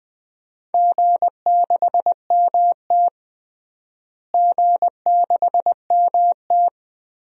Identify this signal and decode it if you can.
Morse code, heard as a keyed tone, about 20 wpm, 700 Hz. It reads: G6MT G6MT
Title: DXCC-100 Call Signs - 4 Characters - Repeated 2x - 20wpm